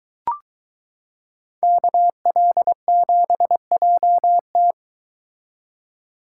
Morse code, heard as a keyed tone, about 23 wpm, 700 Hz. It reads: KL7JT